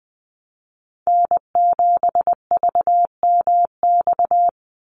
Morse code, heard as a keyed tone, about 20 wpm, 700 Hz. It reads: N7VMX
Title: USA Call Signs 20wpm